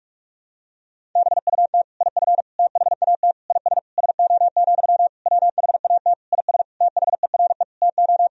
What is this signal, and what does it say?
Morse code, heard as a keyed tone, about 46 wpm, 700 Hz. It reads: BUT IF THAT IS SO, WHAT IS THERE TO
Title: Sentences from Top 200 Words and Abbreviations 45wpm